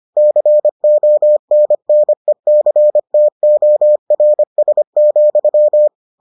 Morse code, heard as a keyed tone, about 25 wpm, 600 Hz. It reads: CONNECTORS,